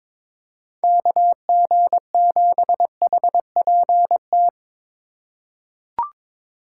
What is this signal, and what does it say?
Morse code, heard as a keyed tone, about 22 wpm, 700 Hz. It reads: KG7HPT
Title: USA Call Signs - 6 Characters - 4x Character Spacing - 22wpm